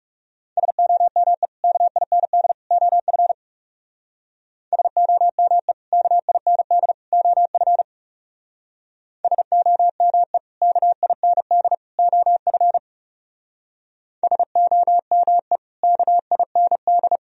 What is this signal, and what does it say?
Morse code, heard as a keyed tone, about 37 wpm, 700 Hz. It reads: SOME KIND OF SOME KIND OF SOME KIND OF SOME KIND
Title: Speed-Racing - Sets of 3 Words 30wpm